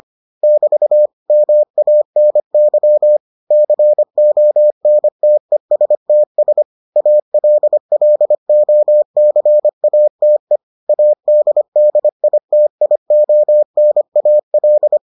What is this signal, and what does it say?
Morse code, heard as a keyed tone, about 25 wpm, 600 Hz. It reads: <BT> MANY CONTESTS ALLOCATE ADDITIONAL